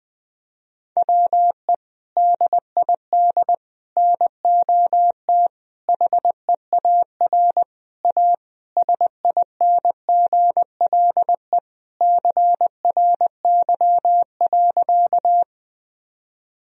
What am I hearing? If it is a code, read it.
Morse code, heard as a keyed tone, about 20 wpm, 700 Hz. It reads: WE DID NOT HEAR A SINGLE CRY.
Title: Sentences from Top 1000 Words - Review Part 4 - 20wpm